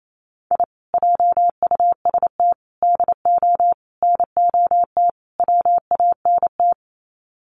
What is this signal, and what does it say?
Morse code, heard as a keyed tone, about 28 wpm, 700 Hz. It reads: I JUST DO NOT WANT
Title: Sets of 5 Words Encore2 28wpm